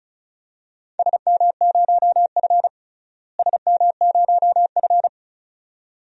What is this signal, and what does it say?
Morse code, heard as a keyed tone, about 36 wpm, 700 Hz. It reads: SM0F SM0F
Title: DXCC-100 Call Signs - 4 Characters - Repeated 2x - 35wpm